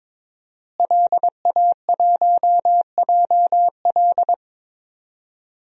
Morse code, heard as a keyed tone, about 22 wpm, 700 Hz. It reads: LA1JL
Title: DXCC-100 Call Signs - 5 Characters - Repeated 2x - 22wpm